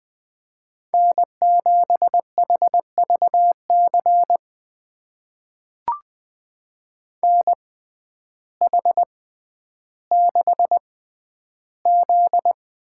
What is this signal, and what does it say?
Morse code, heard as a keyed tone, about 20 wpm, spaced as for 11 wpm, 700 Hz. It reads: N7HVC NH6Z
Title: USA Call Signs - 5 Characters - 6x Character Spacing - 20wpm